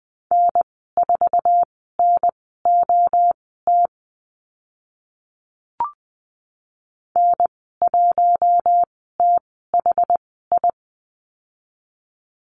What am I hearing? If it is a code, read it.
Morse code, heard as a keyed tone, about 20 wpm, spaced as for 15 wpm, 700 Hz. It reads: N4NOT N1THI
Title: 062 Lesson 11g - Letter h - Call Signs 20wpm